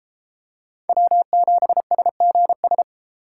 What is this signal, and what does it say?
Morse code, heard as a keyed tone, about 33 wpm, 700 Hz. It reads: W7SGS